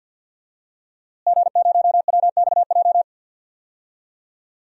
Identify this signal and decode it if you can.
Morse code, heard as a keyed tone, about 52 wpm, 700 Hz. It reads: G0WXJ
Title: DXCC-100 Call Signs - 5 Characters - Repeated 2x - 50wpm